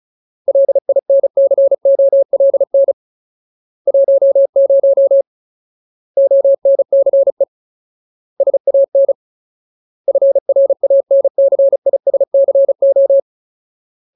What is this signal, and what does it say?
Morse code, heard as a keyed tone, about 34 wpm, 550 Hz. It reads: LINCOLN 10 ONCE SAN FRANCISCO
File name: Ultimate Word List / 35WPM Ultimate